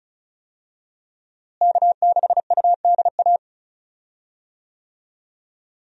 Morse code, heard as a keyed tone, about 34 wpm, 700 Hz. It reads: K6UDA
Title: USA Call Signs - 5 Characters - Repeated 2x - 35wpm